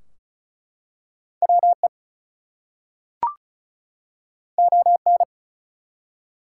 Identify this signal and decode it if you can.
Morse code, heard as a keyed tone, about 36 wpm, 700 Hz. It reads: WE ON
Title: Two Letter Words - 35wpm